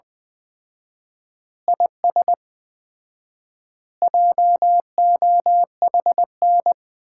Morse code, heard as a keyed tone, about 21 wpm, 700 Hz. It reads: IS JOHN
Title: Common QSO Phrases - 4x Word-Spacing - 20wpm